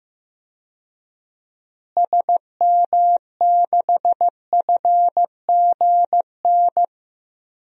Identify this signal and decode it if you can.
Morse code, heard as a keyed tone, about 15 wpm, 700 Hz. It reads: SM6FGN
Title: DXCC-100 Call Signs - 6 Characters - Repeated 3x - 15wpm